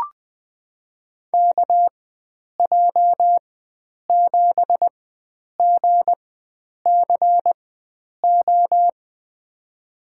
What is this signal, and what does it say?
Morse code, heard as a keyed tone, about 20 wpm, spaced as for 9 wpm, 700 Hz. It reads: KJ7GCO
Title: USA Call Signs - 4x Character Spacing - 20wpm